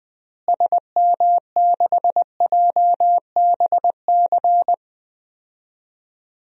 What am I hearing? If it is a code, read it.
Morse code, heard as a keyed tone, about 20 wpm, 700 Hz. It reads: SM6JBC
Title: DXCC-100 Call Signs - 6 Characters - Repeated 2x - 20wpm